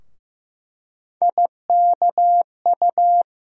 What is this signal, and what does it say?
Morse code, heard as a keyed tone, about 15 wpm, 700 Hz. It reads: IKU
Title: USA Call Sign Postfixes - 3 Letters 15wpm